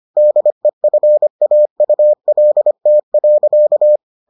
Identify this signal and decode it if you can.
Morse code, heard as a keyed tone, about 25 wpm, 600 Hz. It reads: DEFAULT.